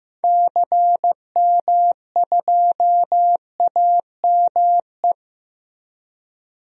Morse code, heard as a keyed tone, about 15 wpm, 700 Hz. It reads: CM2AME
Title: DXCC-100 Call Signs - 6 Characters - Repeated 2x - 15wpm